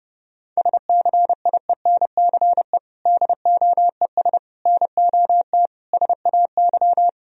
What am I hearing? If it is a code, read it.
Morse code, heard as a keyed tone, about 30 wpm, 700 Hz. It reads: SCIENCE DOES NOT SAY
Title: Sentences from Top 1000 Words - Review Part 2 - 30wpm